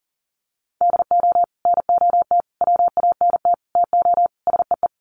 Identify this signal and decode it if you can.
Morse code, heard as a keyed tone, about 40 wpm, 700 Hz. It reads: DO NOT WANT TO SEE
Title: Sets of 5 Words 40wpm